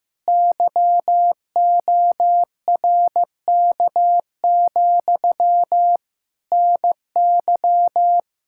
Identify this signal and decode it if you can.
Morse code, heard as a keyed tone, about 15 wpm, 700 Hz. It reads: YORK, NY
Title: K1USN SST QSO Practice 15wpm